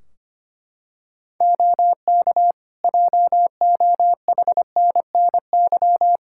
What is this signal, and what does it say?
Morse code, heard as a keyed tone, about 25 wpm, 700 Hz. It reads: OK JOHNNY
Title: Practice QSO - Exchange 3 - More Info - 25wpm